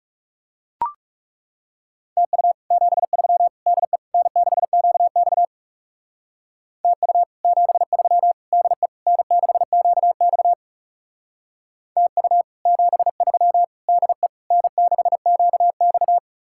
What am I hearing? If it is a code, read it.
Morse code, heard as a keyed tone, about 36 wpm, 700 Hz. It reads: TU 73 DE N6QX TU 73 DE N6QX TU 73 DE N6QX
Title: Speed-Racing Common QSO Phrases - 30wpm